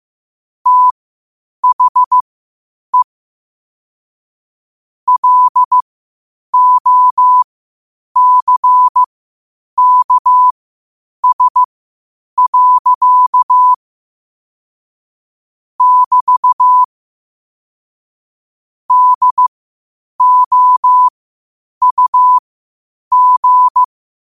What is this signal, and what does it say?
Morse code, heard as a keyed tone, about 15 wpm, spaced as for 8 wpm, 1000 Hz. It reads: THE LOCKS. <BT> DOUG
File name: qotdfiles / 8wpm